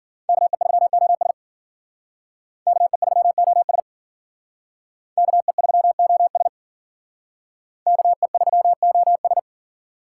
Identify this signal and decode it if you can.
Morse code, heard as a keyed tone, about 53 wpm, 700 Hz. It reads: KE3OS KE3OS KE3OS KE3OS